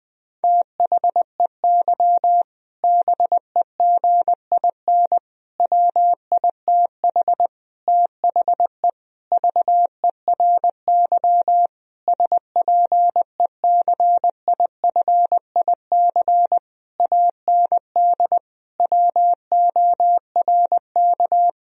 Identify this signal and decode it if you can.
Morse code, heard as a keyed tone, about 20 wpm, 700 Hz. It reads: THEY BEGIN WITH THE VERY SPECIFIC AND WORK
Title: Sentences from Top 1000 Words - Review Part 3 - 20wpm